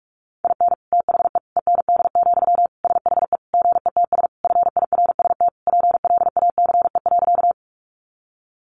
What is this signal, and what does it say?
Morse code, heard as a keyed tone, about 44 wpm, 700 Hz. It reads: IN THE END, SHE GETS FIRST PLACE.
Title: Additional Easy Phrases 45wpm